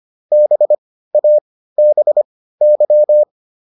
Morse code, heard as a keyed tone, about 25 wpm, spaced as for 15 wpm, 600 Hz. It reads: BABY